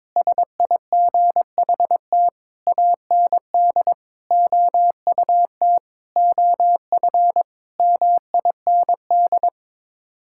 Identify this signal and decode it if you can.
Morse code, heard as a keyed tone, about 22 wpm, 700 Hz. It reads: SIGHT AND OUT OF MIND
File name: practice / 22wpm / Sets of 7 Words - 22wpm